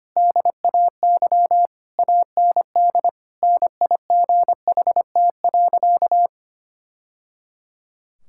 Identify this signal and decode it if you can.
Morse code, heard as a keyed tone, about 25 wpm, 700 Hz. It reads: DAY AND NIGHT.
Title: Sentences from Top 1000 Words - Review Part 6 - 25wpm